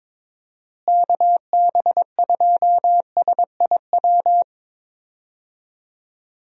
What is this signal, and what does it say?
Morse code, heard as a keyed tone, about 22 wpm, 700 Hz. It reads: KB2SIW
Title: USA Call Signs - 6 Characters - Repeated 3x - 22wpm